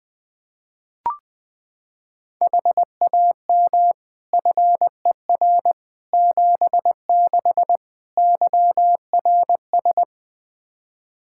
Morse code, heard as a keyed tone, about 20 wpm, 700 Hz. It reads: HAM FER 76 YRS
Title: Practice QSO - Exchange 3 - More Info - 20wpm